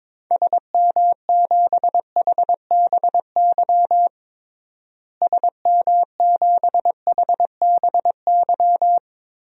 Morse code, heard as a keyed tone, about 22 wpm, 700 Hz. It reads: SM7HBY SM7HBY
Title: DXCC-100 Call Signs - 6 Characters - Repeated 3x - 22wpm